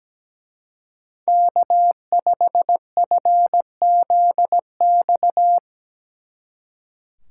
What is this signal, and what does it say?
Morse code, heard as a keyed tone, about 18 wpm, 700 Hz. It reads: K5FZX
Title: Speed Racing - USA 1x3 Call Signs 17wpm